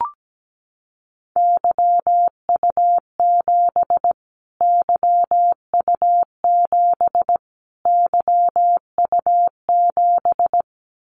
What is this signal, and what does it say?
Morse code, heard as a keyed tone, about 17 wpm, 700 Hz. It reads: YU7 YU7 YU7
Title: Callsign Prefixes for Top 25 DXCC - Repeated 3x - 17wpm